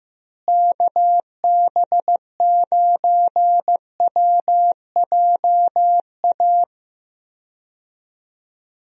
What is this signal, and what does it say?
Morse code, heard as a keyed tone, about 15 wpm, 700 Hz. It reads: KB9WJA
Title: USA Call Signs - 6 Characters - Repeated 3x - 15wpm